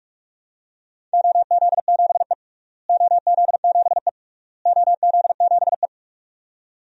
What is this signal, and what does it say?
Morse code, heard as a keyed tone, about 46 wpm, 700 Hz. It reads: OZ7E OZ7E OZ7E
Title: DXCC-100 Call Signs - 4 Characters - Repeated 3x - 45wpm